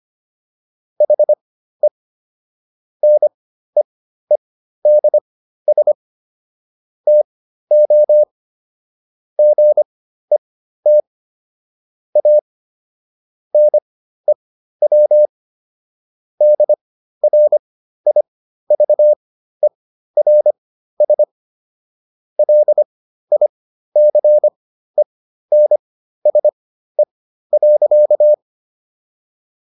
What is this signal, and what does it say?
Morse code, heard as a keyed tone, about 25 wpm, spaced as for 13 wpm, 600 Hz. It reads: HE NEEDS TO GET A NEW DRIVERS LICENSE.